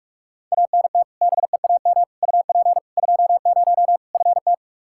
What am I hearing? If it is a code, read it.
Morse code, heard as a keyed tone, about 44 wpm, 700 Hz. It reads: ANT BEAM UP 20 FT
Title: K1USN SST QSO Practice 45wpm